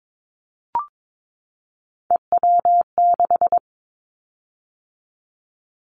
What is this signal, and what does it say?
Morse code, heard as a keyed tone, about 22 wpm, 700 Hz. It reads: EW6
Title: Callsign Prefixes for Top 100 DXCC - 22wpm